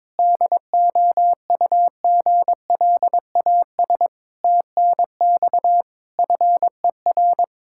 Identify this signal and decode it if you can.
Morse code, heard as a keyed tone, about 21 wpm, 700 Hz. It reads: DOUGLAS TNX FER
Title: Practice QSO - Exchange 4 - Ending - 22wpm